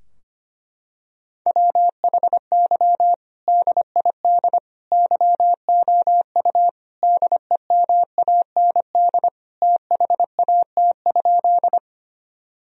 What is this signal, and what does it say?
Morse code, heard as a keyed tone, about 25 wpm, 700 Hz. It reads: WHY DID YOU DEMAND THAT?
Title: Sentences from Top 1000 Words - Review Part 4 - 25wpm